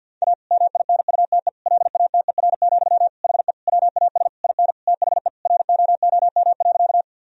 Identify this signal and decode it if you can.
Morse code, heard as a keyed tone, about 50 wpm, 700 Hz. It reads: A MINUTE LATER, HE WAS IN THE ROOM.
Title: Sentences from Top 1000 Words - Review Part 4 - 50wpm